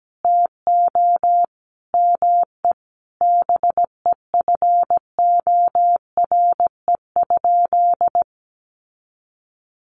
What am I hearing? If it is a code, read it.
Morse code, heard as a keyed tone, about 17 wpm, 700 Hz. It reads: TO ME BEFORE?
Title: Sentences from Top 900 Words 17wpm